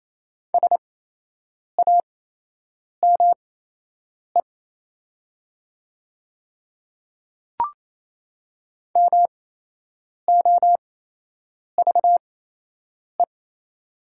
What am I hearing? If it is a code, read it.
Morse code, heard as a keyed tone, about 28 wpm, spaced as for 8 wpm, 700 Hz. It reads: SAME MOVE
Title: 8x Character Spacing - Top 200 Words - 28wpm